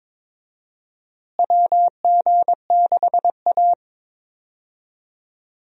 Morse code, heard as a keyed tone, about 21 wpm, 700 Hz. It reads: WG6A
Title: USA Call Signs - 4 Characters - Repeated 2x - 22wpm